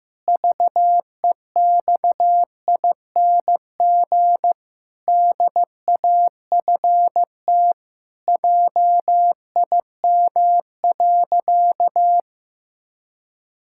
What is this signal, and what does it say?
Morse code, heard as a keyed tone, about 15 wpm, 700 Hz. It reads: VEXING DAFT JIM.
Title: Pangram Challenge v2 - 15wpm